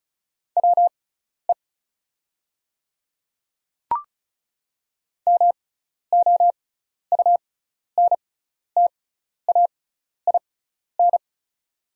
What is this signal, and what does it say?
Morse code, heard as a keyed tone, about 34 wpm, spaced as for 12 wpm, 690 Hz. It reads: WE MOUNTAIN
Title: 6x Character Spacing - Top 300 Words - 35wpm